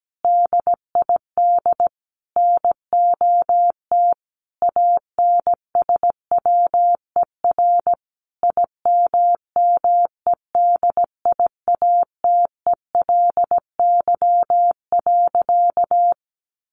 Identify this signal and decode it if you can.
Morse code, heard as a keyed tone, about 17 wpm, 700 Hz. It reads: DID NOT ANSWER IMMEDIATELY.
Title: Sentences from Top 1100 Words 17wpm